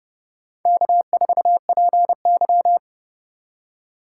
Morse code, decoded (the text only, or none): K4PY